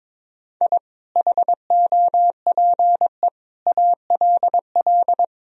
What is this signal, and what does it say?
Morse code, heard as a keyed tone, about 22 wpm, 700 Hz. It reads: I HOPE ALL
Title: Sentences from Top 1000 Words - Review Part 1 - 22wpm